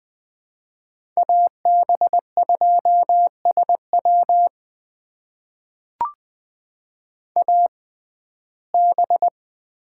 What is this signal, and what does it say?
Morse code, heard as a keyed tone, about 20 wpm, spaced as for 14 wpm, 700 Hz. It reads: AB2SW AB
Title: USA Call Signs - 5 Characters - 6x Character Spacing - 20wpm